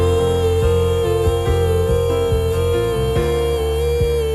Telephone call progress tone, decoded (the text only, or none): busy tone